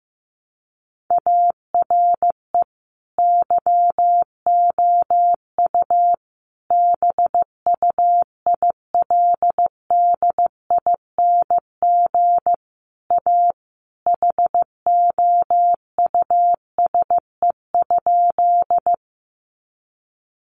Morse code, heard as a keyed tone, about 15 wpm, 700 Hz. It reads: ARE YOU BUILDING A HOUSE?
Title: Sentences from Top 700 Words 15wpm